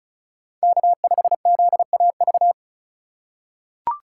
Morse code, heard as a keyed tone, about 36 wpm, 700 Hz. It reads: K5ZAV
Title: USA Call Signs - 5 Characters - 3x Character Spacing - 35wpm